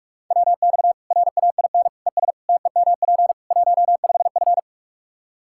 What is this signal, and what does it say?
Morse code, heard as a keyed tone, about 46 wpm, 700 Hz. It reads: WX RAIN ES TEMP 15F
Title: Practice QSO - Exchange 2 - Info - 45wpm